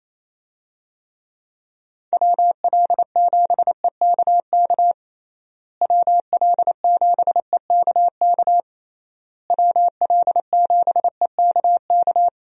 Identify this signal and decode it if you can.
Morse code, heard as a keyed tone, about 28 wpm, 700 Hz. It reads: WL7EKK WL7EKK WL7EKK
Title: USA Call Signs - 6 Characters - Repeated 3x - 28wpm